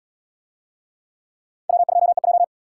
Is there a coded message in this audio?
Yes: Morse code from 1.7 to 2.4 s, heard as a keyed tone, about 77 wpm, 700 Hz. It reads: W2EJ